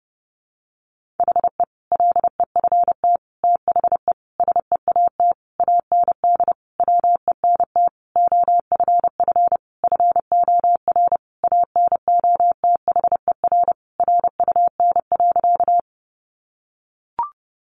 Morse code, heard as a keyed tone, about 28 wpm, 700 Hz. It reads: HE LEFT THE SEAT AND WENT OFF FOR ANOTHER RUN.
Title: Sentences from Top 1500 Words 30wpm